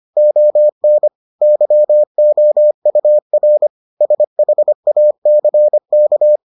Morse code, heard as a keyed tone, about 25 wpm, 600 Hz. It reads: ON YOUR SHACK